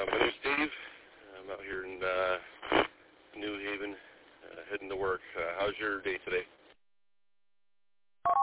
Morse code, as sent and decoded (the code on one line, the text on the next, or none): none